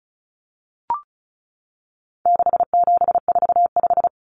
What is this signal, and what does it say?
Morse code, heard as a keyed tone, about 34 wpm, 700 Hz. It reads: 6745
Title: Numbers - 4-Digit 35wpm